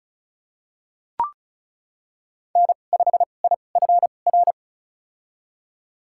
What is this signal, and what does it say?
Morse code, heard as a keyed tone, about 34 wpm, spaced as for 25 wpm, 700 Hz. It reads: N5IFR
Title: USA Call Signs - 5 Characters - 2x Character Spacing - 35wpm